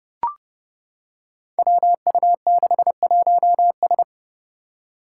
Morse code, heard as a keyed tone, about 30 wpm, 700 Hz. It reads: WU61S
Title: Five Challenging Letter-Number Combinations 30wpm